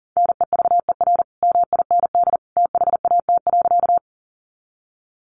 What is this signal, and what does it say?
Morse code, heard as a keyed tone, about 40 wpm, 700 Hz. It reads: NEVER MIND THAT.
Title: Sentences from Top 500 Words 40wpm